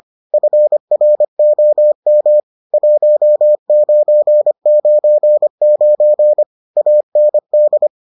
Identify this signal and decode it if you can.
Morse code, heard as a keyed tone, about 25 wpm, 600 Hz. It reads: FROM 1999 AND